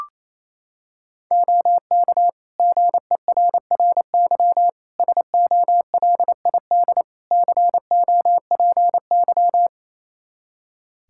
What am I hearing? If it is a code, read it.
Morse code, heard as a keyed tone, about 27 wpm, 700 Hz. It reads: OK GERRY SOLID COPY
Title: Practice QSO - Exchange 3 - More Info - 28wpm